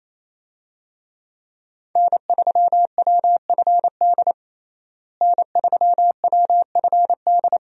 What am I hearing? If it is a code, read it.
Morse code, heard as a keyed tone, about 28 wpm, 700 Hz. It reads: N3WFD N3WFD
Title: USA Call Signs - 5 Characters - Repeated 2x - 28wpm